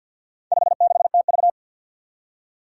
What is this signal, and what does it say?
Morse code, heard as a keyed tone, about 49 wpm, 700 Hz. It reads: 56TV
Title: Four Challenging Letter-Number Combinations 50wpm